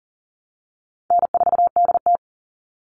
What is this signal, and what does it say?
Morse code, heard as a keyed tone, about 41 wpm, 700 Hz. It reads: N4DT
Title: Easy Call Signs - 40wpm